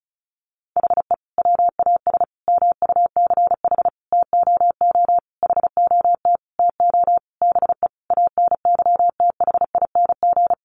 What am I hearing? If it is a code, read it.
Morse code, heard as a keyed tone, about 34 wpm, 700 Hz. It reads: HE WAS MUCH TOO HOT TO BE ANYTHING
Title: Sentences from Top 1200 Words 35wpm